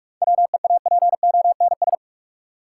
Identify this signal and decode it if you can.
Morse code, heard as a keyed tone, about 46 wpm, 700 Hz. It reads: WEAPONS
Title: Sets of 5 Words Encore - New Word Drill - 45wpm